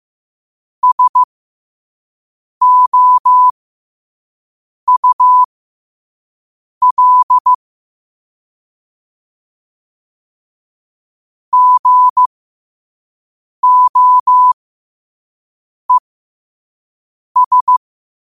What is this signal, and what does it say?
Morse code, heard as a keyed tone, about 15 wpm, spaced as for 5 wpm, 1000 Hz. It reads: SOUL GOES
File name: qotdfiles / 5wpm